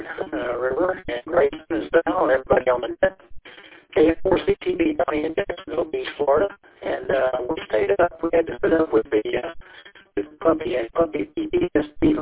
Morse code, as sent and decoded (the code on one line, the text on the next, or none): none